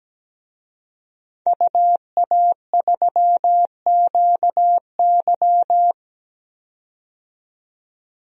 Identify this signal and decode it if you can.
Morse code, heard as a keyed tone, about 17 wpm, 700 Hz. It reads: UA3QY
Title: DXCC-100 Call Signs - 5 Characters - 17wpm